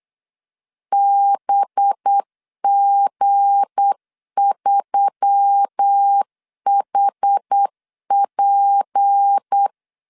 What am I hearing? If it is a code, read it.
Morse code, heard as a keyed tone, about 8 wpm, 800 Hz. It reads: BG3HP
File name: LEZIONI / 7 GF9Z / 7 G 8wpm 3118